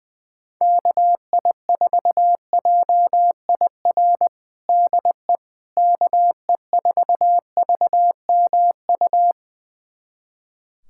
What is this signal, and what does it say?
Morse code, heard as a keyed tone, about 20 wpm, 700 Hz. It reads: KI4JIR DE KE4VMU
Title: USA Callsign DE Callsign - 6 Characters - 20wpm